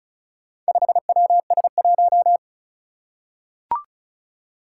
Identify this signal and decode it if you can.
Morse code, heard as a keyed tone, about 36 wpm, 700 Hz. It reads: 5WS1
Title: Four Challenging Letter-Number Combinations 35wpm